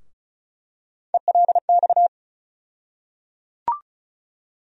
Morse code, heard as a keyed tone, about 36 wpm, 700 Hz. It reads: ELX
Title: USA Call Sign Postfixes - 3 Letters 35wpm